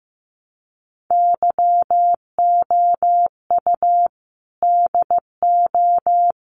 Morse code, heard as a keyed tone, about 15 wpm, 700 Hz. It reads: YOU DO
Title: Sets of 4 Words 15wpm